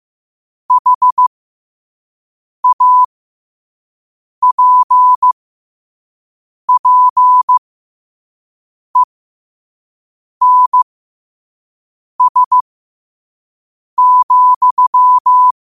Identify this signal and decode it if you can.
Morse code, heard as a keyed tone, about 15 wpm, spaced as for 5 wpm, 1000 Hz. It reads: HAPPENS,
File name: qotdfiles / 5wpm